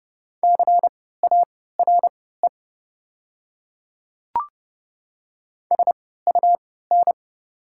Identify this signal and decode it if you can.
Morse code, heard as a keyed tone, about 30 wpm, spaced as for 17 wpm, 690 Hz. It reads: CARE SUN
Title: 3x Character Spacing - Top 300 Words - 30wpm